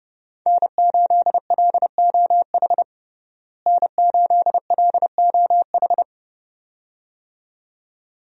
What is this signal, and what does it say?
Morse code, heard as a keyed tone, about 32 wpm, 700 Hz. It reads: N8LOH N8LOH
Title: USA Call Signs - 5 Characters - Repeated 2x - 30wpm